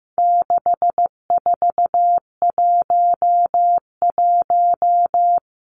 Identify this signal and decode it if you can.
Morse code, heard as a keyed tone, about 15 wpm, 700 Hz. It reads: 6411
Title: Numbers - 4-Digit 15wpm